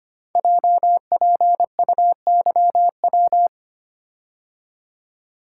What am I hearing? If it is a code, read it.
Morse code, heard as a keyed tone, about 25 wpm, 700 Hz. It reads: JPUYW
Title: Code Groups - 5 Letters - 25wpm